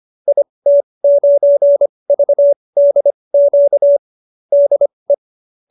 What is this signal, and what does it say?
Morse code, heard as a keyed tone, about 25 wpm, spaced as for 20 wpm, 570 Hz. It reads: IT9VDQ DE
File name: qso215b 20